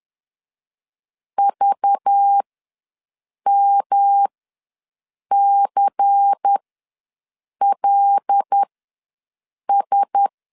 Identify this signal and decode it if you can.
Morse code, heard as a keyed tone, about 10 wpm, spaced as for 6 wpm, 800 Hz. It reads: VMCLS
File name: LEZIONI / 6 PKX0 / 6 P ws 10wpm 3751